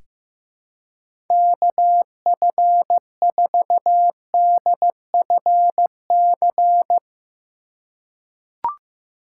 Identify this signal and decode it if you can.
Morse code, heard as a keyed tone, about 15 wpm, 700 Hz. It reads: KF4DFC
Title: USA Call Signs - 6 Characters - 2x Character Spacing - 15wpm